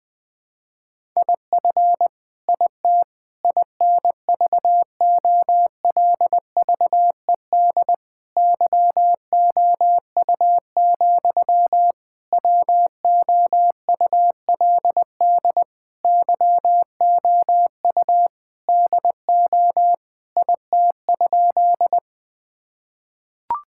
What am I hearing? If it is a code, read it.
Morse code, heard as a keyed tone, about 20 wpm, 700 Hz. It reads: IF IT INVOLVED YOU, WOULD YOU DO IT?
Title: Sentences from Top 1000 Words - Review Part 5 - 20wpm